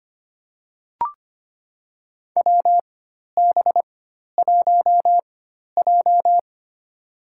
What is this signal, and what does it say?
Morse code, heard as a keyed tone, about 25 wpm, spaced as for 12 wpm, 700 Hz. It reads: WB1J
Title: USA Call Signs - 4 Characters - 4x Character Spacing - 25wpm